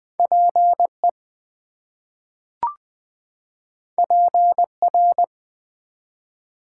Morse code, heard as a keyed tone, about 20 wpm, 700 Hz. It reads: PE PR
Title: ARRL Sections v2 - 20wpm